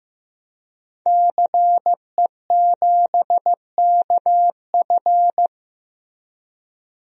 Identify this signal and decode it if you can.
Morse code, heard as a keyed tone, about 15 wpm, 700 Hz. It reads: CE7KF
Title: DXCC-100 Call Signs - 5 Characters - Repeated 3x - 15wpm